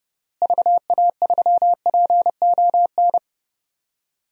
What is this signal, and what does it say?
Morse code, heard as a keyed tone, about 30 wpm, 700 Hz. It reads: VA3PON
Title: DXCC-100 Call Signs - 6 Characters - Repeated 3x - 30wpm